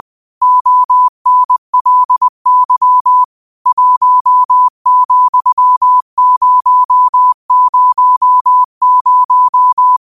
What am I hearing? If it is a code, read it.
Morse code, heard as a keyed tone, about 20 wpm, 1000 Hz. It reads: ONLY 1,000